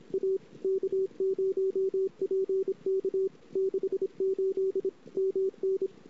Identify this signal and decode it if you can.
Morse code, heard as a keyed tone, about 35 wpm, 400 Hz. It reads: A K0PK 68 MN
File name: CW / 2008-ss-cw-k5tr-op-n5rz -21:00